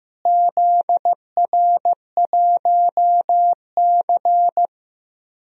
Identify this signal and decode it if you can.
Morse code, heard as a keyed tone, about 15 wpm, 700 Hz. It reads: ZR1C
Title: DXCC-100 Call Signs - 4 Characters - Repeated 3x - 15wpm